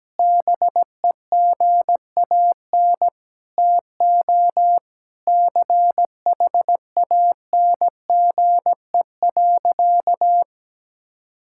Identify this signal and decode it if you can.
Morse code, heard as a keyed tone, about 17 wpm, 700 Hz. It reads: BEGAN TO CHANGE.
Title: Sentences from Top 1600 Words 17wpm